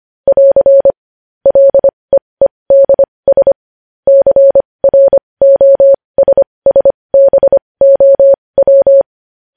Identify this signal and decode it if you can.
Morse code, heard as a keyed tone, about 25 wpm, spaced as for 20 wpm, 550 Hz. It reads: <AR> LEEDS CROSSBOW
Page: news headlines in morse code at 20 wpm